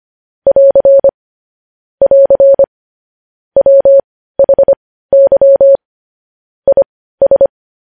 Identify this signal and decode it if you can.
Morse code, heard as a keyed tone, about 25 wpm, spaced as for 15 wpm, 550 Hz. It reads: <AR> <AR> WHY IS
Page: news-headlines-in-morse-code-at-15-wpm